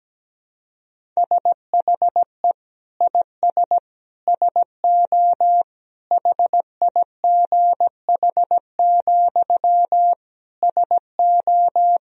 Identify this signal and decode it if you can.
Morse code, heard as a keyed tone, about 17 wpm, 700 Hz. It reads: SHE IS SO HIGH, SO
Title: Sentences from Top 1000 Words - Review Part 1 - 17wpm